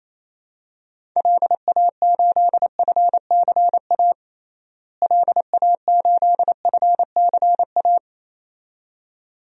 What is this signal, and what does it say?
Morse code, heard as a keyed tone, about 27 wpm, 700 Hz. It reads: LA8FCA LA8FCA
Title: DXCC-100 Call Signs - 6 Characters - Repeated 2x - 28wpm